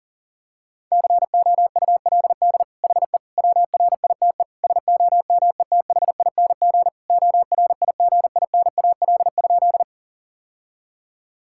Morse code, heard as a keyed tone, about 40 wpm, 700 Hz. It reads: COULD HE WRITE SOMETHING ORIGINAL?
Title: Sentences from Top 1000 Words - Review Part 6 - 40wpm